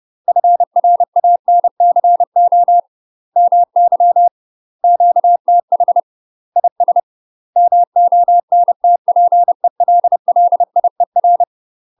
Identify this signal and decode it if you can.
Morse code, heard as a keyed tone, about 30 wpm, 700 Hz. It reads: FRANCO MY QTH IS MONTPELLIER
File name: UK MASTER QSOs TRAINING / 30WPM / UK MASTER QSOs 0011